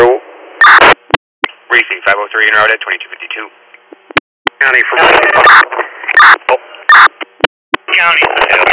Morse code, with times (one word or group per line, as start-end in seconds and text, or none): none